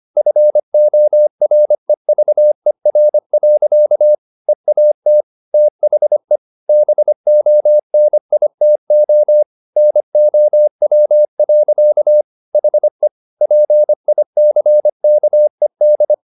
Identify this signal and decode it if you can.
Morse code, heard as a keyed tone, about 25 wpm, 600 Hz. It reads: FOREVER. EAT THE BONITO NOW. HE PICKED
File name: the old man and the sea part 1 25wpm